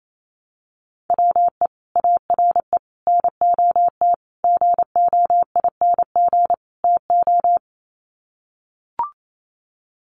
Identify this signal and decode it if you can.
Morse code, heard as a keyed tone, about 28 wpm, 700 Hz. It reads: WE ARE NOT GOING TO
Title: Sets of 5 Words 28wpm